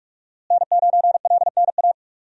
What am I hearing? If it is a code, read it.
Morse code, heard as a keyed tone, about 43 wpm, 700 Hz. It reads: N9LNA
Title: USA Call Signs - 5 Characters - Repeated 2x - 45wpm